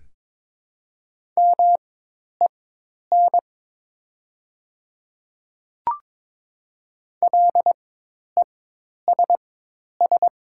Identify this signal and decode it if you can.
Morse code, heard as a keyed tone, about 21 wpm, spaced as for 10 wpm, 700 Hz. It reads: MEN LESS
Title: 4x Character Spacing - Top 500 Words - 22wpm